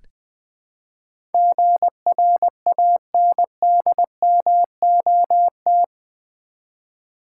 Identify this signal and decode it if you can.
Morse code, heard as a keyed tone, about 20 wpm, 700 Hz. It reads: GRANDMOT
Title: Build A Word - 4 Letters At A Time - 11-Letter Words - 20wpm